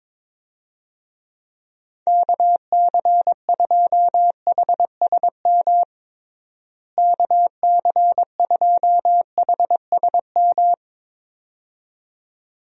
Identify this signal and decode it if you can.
Morse code, heard as a keyed tone, about 22 wpm, 700 Hz. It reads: KC2HSM KC2HSM
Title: USA Call Signs - 6 Characters - Repeated 2x - 22wpm